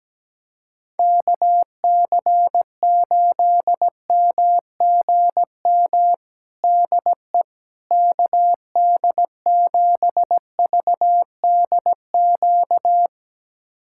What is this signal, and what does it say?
Morse code, heard as a keyed tone, about 17 wpm, 700 Hz. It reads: KC8MGM DE KD7VDQ
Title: USA Callsign DE Callsign - 6 Characters - 17wpm